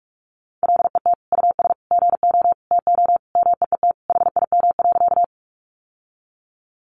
Morse code, heard as a keyed tone, about 44 wpm, 700 Hz. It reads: LET US GO TO MEET HIM.
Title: Sentences from Top 1100 Words 45wpm